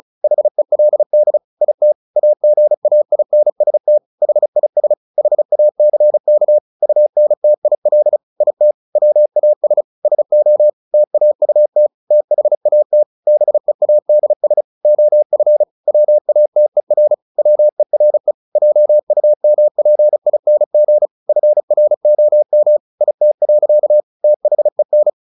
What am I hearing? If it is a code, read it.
Morse code, heard as a keyed tone, about 34 wpm, 600 Hz. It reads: HELD IT AGAINST HIS HACK UNTIL IT WAS SO TAUT THAT BEADS OF WATER WERE JUMPING FROM IT. THEN
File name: the old man and the sea / the old man and the sea 1 35wpm